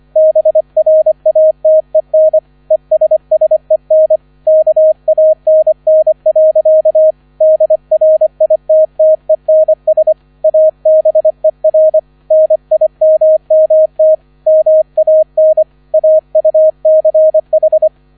Morse code, heard as a keyed tone, about 25 wpm, 630 Hz. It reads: BRATEN ESSEN KANN. DRITTENS ABER NIMMT MAN AUCH